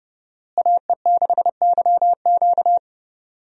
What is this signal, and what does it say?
Morse code, heard as a keyed tone, about 30 wpm, 700 Hz. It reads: AE6YQ
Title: USA Call Signs - 5 Characters - Repeated 3x - 30wpm